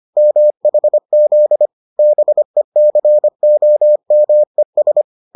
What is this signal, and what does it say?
Morse code, heard as a keyed tone, about 25 wpm, 600 Hz. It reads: MHZ BECOMES